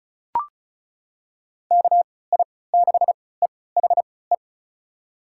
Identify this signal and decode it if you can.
Morse code, heard as a keyed tone, about 36 wpm, spaced as for 20 wpm, 700 Hz. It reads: KI6EHE
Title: USA Call Signs - 6 Characters - 3x Character Spacing - 35wpm